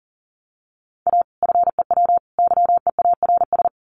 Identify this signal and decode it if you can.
Morse code, heard as a keyed tone, about 40 wpm, 700 Hz. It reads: A FEW YEARS
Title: Sets of 3 Words 40wpm